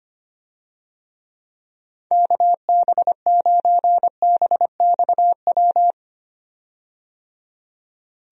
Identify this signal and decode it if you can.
Morse code, heard as a keyed tone, about 25 wpm, 700 Hz. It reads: KB9BXW
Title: USA Call Signs - 6 Characters - 25wpm